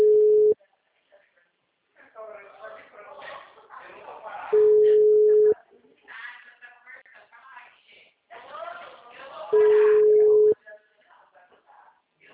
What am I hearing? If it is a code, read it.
Ringback tone (Europe, 425 Hz), with speech.